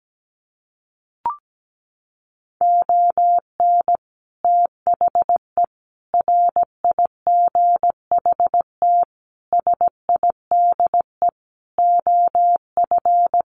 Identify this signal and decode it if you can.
Morse code, heard as a keyed tone, about 17 wpm, 700 Hz. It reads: ON THE RIGHT SIDE OF
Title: Sets of 5 Words Encore 17wpm